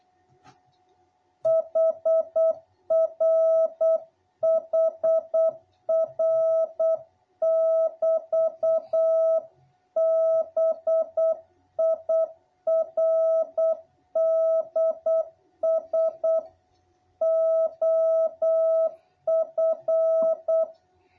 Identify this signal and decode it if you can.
Morse code, heard as a keyed tone, about 8 wpm, 660 Hz. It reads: HR HR<BT>BIRDS OF
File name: jh8cbh / 2021cw